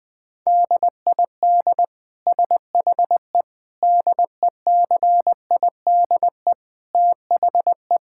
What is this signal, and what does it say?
Morse code, heard as a keyed tone, about 20 wpm, 700 Hz. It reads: DID SHE DECIDE THE
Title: Sentences from Top 1600 Words 20wpm